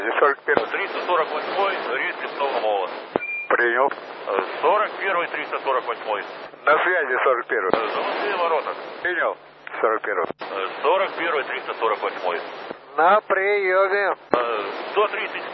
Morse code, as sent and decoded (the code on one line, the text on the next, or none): none